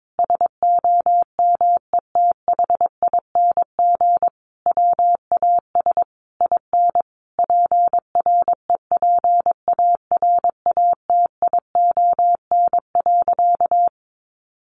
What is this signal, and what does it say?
Morse code, heard as a keyed tone, about 22 wpm, 700 Hz. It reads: SOMETHING WAS IN PREPARATION.
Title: Sentences from Top 1400 Words 22wpm